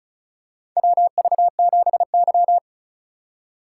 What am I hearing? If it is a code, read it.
Morse code, heard as a keyed tone, about 34 wpm, 700 Hz. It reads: WV7Y
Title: USA Call Signs - 6x Character Spacing - 35wpm